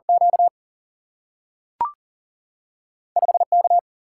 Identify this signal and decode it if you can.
Morse code, heard as a keyed tone, about 40 wpm, 700 Hz. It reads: Q 5 K
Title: Single-Letter Single-Number 40wpm